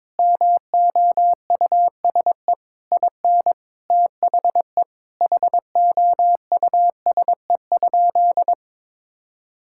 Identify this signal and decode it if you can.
Morse code, heard as a keyed tone, about 22 wpm, 700 Hz. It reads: MOUSE IN THE HOUSE?
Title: Sentences from Top 1000 Words - Review Part 5 - 22wpm